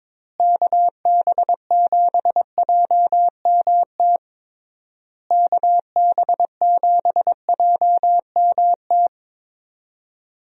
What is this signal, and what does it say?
Morse code, heard as a keyed tone, about 22 wpm, 700 Hz. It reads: KB7JMT KB7JMT
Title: USA Call Signs - 6 Characters - Repeated 3x - 22wpm